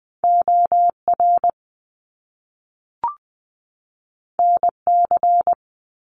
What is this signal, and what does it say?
Morse code, heard as a keyed tone, about 20 wpm, 700 Hz. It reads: O R N C
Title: Top Two Letter Combinations in Words 20wpm